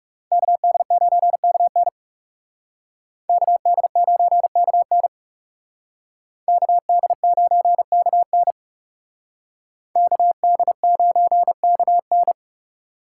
Morse code, heard as a keyed tone, about 34 wpm, 700 Hz. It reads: KD9KN KD9KN KD9KN KD9KN
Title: Speed Racing - USA 2x2 Call Signs 30wpm